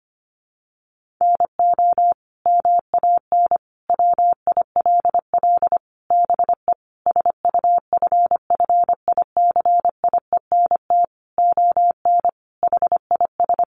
Morse code, heard as a keyed tone, about 25 wpm, 700 Hz. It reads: NO MAN WILL BE SUFFICIENT ON HIS
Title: Sentences from Top 1400 Words 25wpm